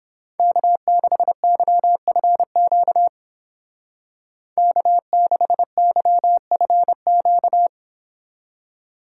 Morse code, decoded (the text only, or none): K6YFQ K6YFQ